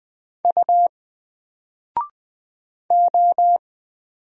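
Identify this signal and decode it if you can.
Morse code, heard as a keyed tone, about 20 wpm, 700 Hz. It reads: U O
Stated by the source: Single letter and number Morse drill at 20 wpm, from N4LCD.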